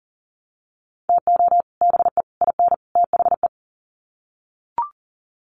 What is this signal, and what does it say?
Morse code, heard as a keyed tone, about 40 wpm, 700 Hz. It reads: TO BE IN THE
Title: Sets of 4 Words 40wpm